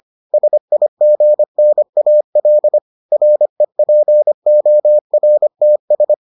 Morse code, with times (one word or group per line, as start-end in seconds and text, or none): 0.3-2.8 SIGNAL
3.1-6.1 REPORTS